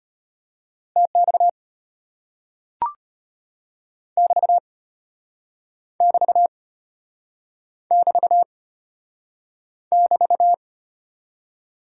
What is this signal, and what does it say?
Morse code, heard as a keyed tone, about 33 wpm, 700 Hz. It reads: TX <BT> <BT> <BT> <BT>